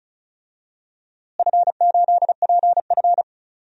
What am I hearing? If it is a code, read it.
Morse code, heard as a keyed tone, about 34 wpm, 700 Hz. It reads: F8PF
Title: DXCC-100 Call Signs - 4 Characters - Repeated 2x - 35wpm